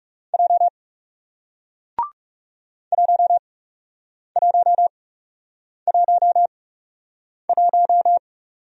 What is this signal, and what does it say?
Morse code, heard as a keyed tone, about 39 wpm, spaced as for 8 wpm, 700 Hz. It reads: J 1111